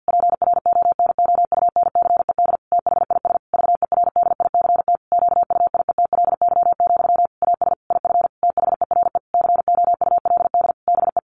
Morse code, heard as a keyed tone, about 50 wpm, 700 Hz. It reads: PRONOUNCED THIS VERDICT QUIETLY, AS IF THERE COULD BE
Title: A Sound Of Thunder 50wpm